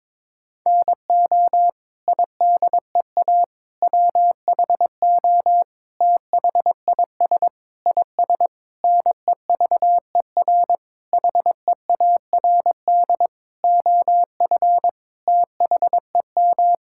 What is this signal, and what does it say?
Morse code, heard as a keyed tone, about 22 wpm, 700 Hz. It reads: NO IDEA WHO THIS IS NEVER HEARD OF THEM
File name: Sets of 9 Words - 22wpm